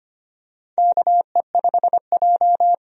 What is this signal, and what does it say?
Morse code, heard as a keyed tone, about 25 wpm, 700 Hz. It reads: KE5J
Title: USA Call Signs - 4 Characters - 25wpm